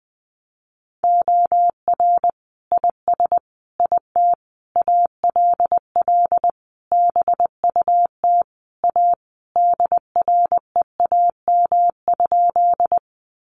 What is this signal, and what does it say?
Morse code, heard as a keyed tone, about 19 wpm, 700 Hz. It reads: OR IS IT ALL BUT A DREAM?
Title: Sentences from Top 1500 Words 20wpm